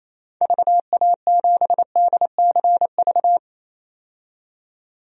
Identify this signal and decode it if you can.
Morse code, heard as a keyed tone, about 28 wpm, 700 Hz. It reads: VA7DCV
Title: DXCC-100 Call Signs - 6 Characters - 28wpm